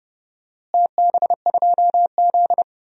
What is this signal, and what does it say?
Morse code, heard as a keyed tone, about 32 wpm, 700 Hz. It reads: TB2Z
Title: DXCC-100 Call Signs - 4 Characters - Repeated 2x - 30wpm